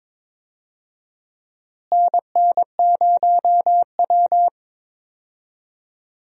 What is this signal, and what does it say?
Morse code, heard as a keyed tone, about 22 wpm, 700 Hz. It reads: NN0W
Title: USA Call Signs - 4 Characters - Repeated 3x - 22wpm